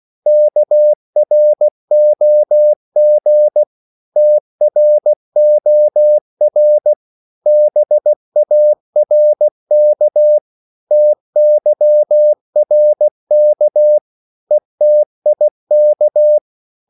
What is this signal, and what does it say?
Morse code, heard as a keyed tone, about 16 wpm, 600 Hz. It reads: KROG TROR BARK TYRK ETIK
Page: Korte danske ord 16wpm